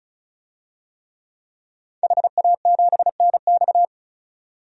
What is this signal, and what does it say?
Morse code, heard as a keyed tone, about 34 wpm, 700 Hz. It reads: HA7NX
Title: DXCC-100 Call Signs - 5 Characters - 35wpm